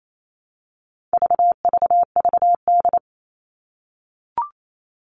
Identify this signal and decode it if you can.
Morse code, heard as a keyed tone, about 28 wpm, 700 Hz. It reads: VVVD
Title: All Cut Numbers - 4-Digit 28wpm